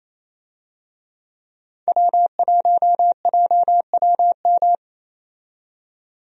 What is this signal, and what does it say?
Morse code, heard as a keyed tone, about 28 wpm, 700 Hz. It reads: W1JWM
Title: USA Call Signs - 5 Characters - 28wpm